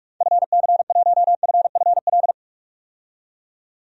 Morse code, heard as a keyed tone, about 46 wpm, 700 Hz. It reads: FC1FFL
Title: DXCC-100 Call Signs - 6 Characters - 45wpm